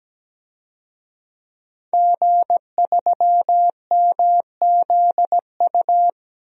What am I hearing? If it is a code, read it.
Morse code, heard as a keyed tone, about 17 wpm, 700 Hz. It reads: G3MZU
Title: DXCC-100 Call Signs - 5 Characters - 17wpm